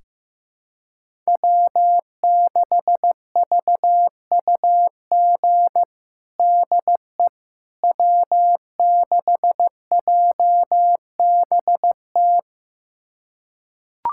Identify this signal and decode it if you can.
Morse code, heard as a keyed tone, about 15 wpm, 700 Hz. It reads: W6VUG DE W6JBT
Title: USA Callsign DE Callsign - 5 Characters - 15wpm